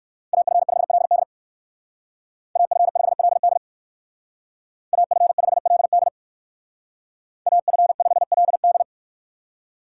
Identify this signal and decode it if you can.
Morse code, heard as a keyed tone, about 58 wpm, 700 Hz. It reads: AF5LD AF5LD AF5LD AF5LD